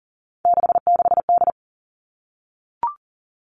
Morse code, heard as a keyed tone, about 38 wpm, 700 Hz. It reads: 66D
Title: Three Challenging Letter-Number Combinations 40wpm